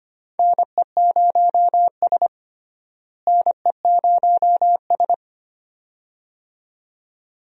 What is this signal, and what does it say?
Morse code, heard as a keyed tone, about 25 wpm, 700 Hz. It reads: NE0S NE0S
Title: USA Call Signs - 4 Characters - Repeated 2x - 25wpm